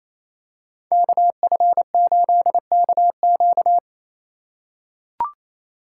Morse code, heard as a keyed tone, about 27 wpm, 700 Hz. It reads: KF8KQ
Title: USA Call Signs - 2x Character Spacing - 28wpm